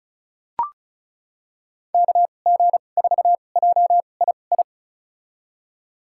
Morse code, heard as a keyed tone, about 36 wpm, spaced as for 25 wpm, 700 Hz. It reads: KG4JII
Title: USA Call Signs - 6 Characters - 2x Character Spacing - 35wpm